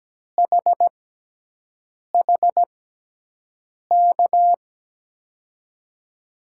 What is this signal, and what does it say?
Morse code, heard as a keyed tone, about 17 wpm, spaced as for 6 wpm, 700 Hz. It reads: HHK
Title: USA Call Signs - 6 Characters - 6x Character Spacing - 17wpm